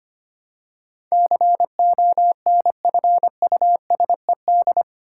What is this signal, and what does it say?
Morse code, heard as a keyed tone, about 25 wpm, 700 Hz. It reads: CONFUSED
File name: Eight Letter Words - 25wpm